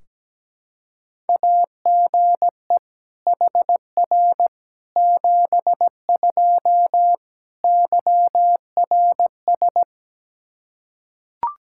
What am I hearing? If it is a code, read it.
Morse code, heard as a keyed tone, about 17 wpm, 700 Hz. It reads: AGE HR 72 YRS
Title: Practice QSO - Exchange 3 - More Info - 17wpm